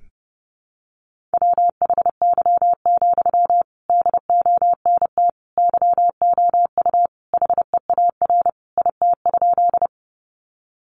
Morse code, heard as a keyed tone, about 30 wpm, 700 Hz. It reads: WHY, DONT YOU HEAR IT?
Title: Sentences from Top 300 Words 30wpm